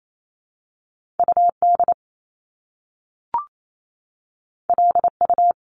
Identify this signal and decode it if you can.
Morse code, heard as a keyed tone, about 28 wpm, 700 Hz. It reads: U D L U
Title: Top Two Letter Combinations in Words 28wpm